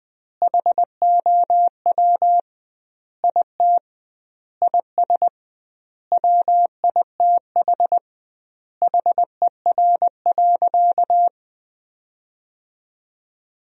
Morse code, heard as a keyed tone, about 20 wpm, 700 Hz. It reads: HOW IT IS WITH HER.
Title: 2x Word-Spacing - Sentences from Top 100 Words - 20wpm